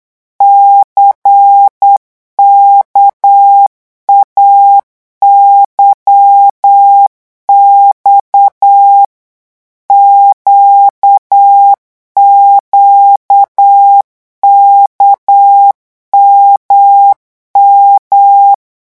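Morse code, heard as a keyed tone, about 8 wpm, 800 Hz. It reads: CKAYX QQKMM
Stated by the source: Morse code revision lesson at 8 words per minute, from a French amateur radio course.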